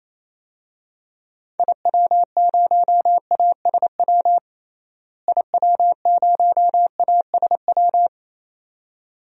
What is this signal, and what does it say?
Morse code, heard as a keyed tone, about 27 wpm, 700 Hz. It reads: IW0ASW IW0ASW
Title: DXCC-100 Call Signs - 6 Characters - Repeated 2x - 28wpm